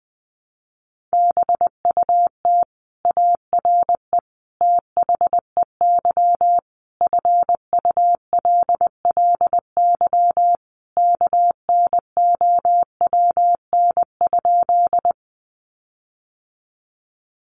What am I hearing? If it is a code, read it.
Morse code, heard as a keyed tone, about 20 wpm, 700 Hz. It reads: BUT ARE THEY FULLY KNOWN?
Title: Sentences from Top 1300 Words 20wpm